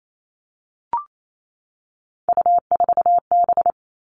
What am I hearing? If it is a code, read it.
Morse code, heard as a keyed tone, about 28 wpm, 700 Hz. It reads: U4B
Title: Three Challenging Letter-Number Combinations 28wpm